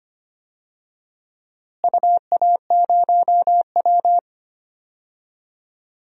Morse code, heard as a keyed tone, about 25 wpm, 700 Hz. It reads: UA0W
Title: DXCC-100 Call Signs - 4 Characters - Repeated 3x - 25wpm